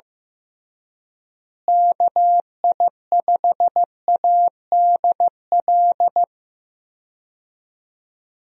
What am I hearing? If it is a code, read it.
Morse code, heard as a keyed tone, about 15 wpm, 700 Hz. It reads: KI5ADL
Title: USA Call Signs - 6 Characters - Repeated 3x - 15wpm